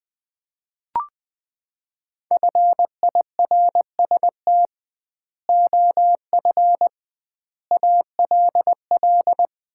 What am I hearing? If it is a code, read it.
Morse code, heard as a keyed tone, about 20 wpm, 700 Hz. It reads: FIRST OF ALL
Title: 2x Word-Spacing - Sets of 3 Words - 20wpm